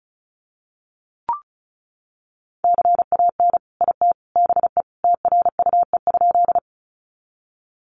Morse code, heard as a keyed tone, about 34 wpm, 700 Hz. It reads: CAN IT BE TRUE?
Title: Sentences from Top 400 Words 35wpm